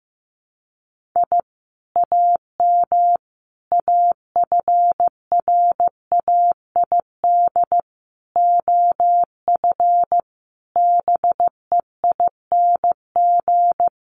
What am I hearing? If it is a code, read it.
Morse code, heard as a keyed tone, about 15 wpm, 700 Hz. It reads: I AM AFRAID OF BEING
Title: Sentences from Top 1500 Words 15wpm